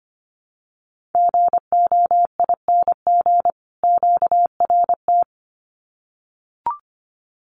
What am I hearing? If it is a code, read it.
Morse code, heard as a keyed tone, about 25 wpm, 700 Hz. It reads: GOING QRT
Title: Common QSO Phrases 25wpm